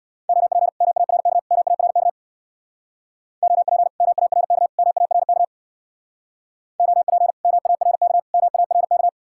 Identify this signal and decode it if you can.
Morse code, heard as a keyed tone, about 60 wpm, 700 Hz. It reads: OP MARK MARK OP MARK MARK OP MARK MARK
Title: Speed-Racing Common QSO Phrases - 50wpm